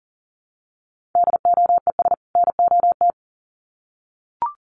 Morse code, heard as a keyed tone, about 40 wpm, 700 Hz. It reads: DOES NOT
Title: Sets of 2 Words 40wpm